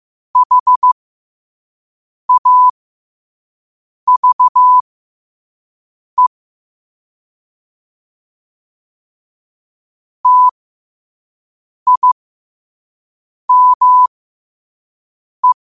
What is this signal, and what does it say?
Morse code, heard as a keyed tone, about 15 wpm, spaced as for 5 wpm, 1000 Hz. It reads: HAVE TIME
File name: qotdfiles / 5wpm